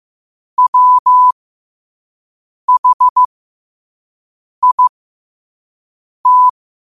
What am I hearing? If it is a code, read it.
Morse code, heard as a keyed tone, about 15 wpm, spaced as for 5 wpm, 1000 Hz. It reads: WHIT
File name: qotdfiles / 5wpm